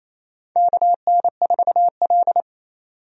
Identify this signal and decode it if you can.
Morse code, heard as a keyed tone, about 28 wpm, 700 Hz. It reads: KN4L
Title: USA Call Signs - 4 Characters - 28wpm